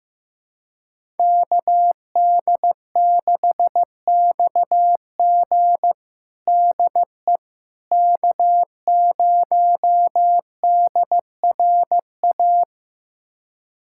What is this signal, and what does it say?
Morse code, heard as a keyed tone, about 16 wpm, 700 Hz. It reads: KD6XG DE K0DRA
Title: USA Callsign DE Callsign - 5 Characters - 15wpm